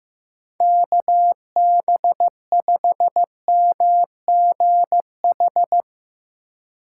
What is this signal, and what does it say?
Morse code, heard as a keyed tone, about 15 wpm, 700 Hz. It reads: KB5MGH
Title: USA Call Signs - 6 Characters - 2x Character Spacing - 15wpm